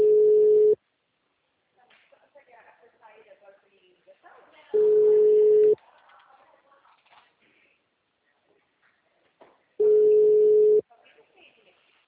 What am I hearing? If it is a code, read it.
Ringback tone (Europe, 425 Hz), with speech.
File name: out- -2046- - - .31311